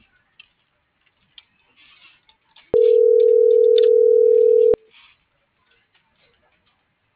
Ringback tone (North America), 440 + 480 Hz.
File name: CENTRAL SOTOVA / out alejandro.rodriguez - .48998